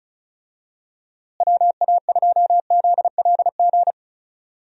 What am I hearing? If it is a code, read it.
Morse code, heard as a keyed tone, about 34 wpm, 700 Hz. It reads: WA2ZLG
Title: USA Call Signs - 6 Characters - Repeated 3x - 35wpm